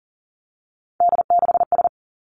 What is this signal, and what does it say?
Morse code, heard as a keyed tone, about 42 wpm, 700 Hz. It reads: D6S
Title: Three Challenging Letter-Number Combinations 40wpm